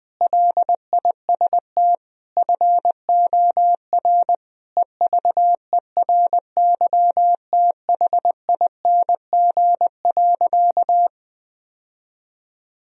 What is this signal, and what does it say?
Morse code, heard as a keyed tone, about 21 wpm, 700 Hz. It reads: LIST FOR EVERYTHING.
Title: Sentences from Top 1000 Words - Review Part 2 - 20wpm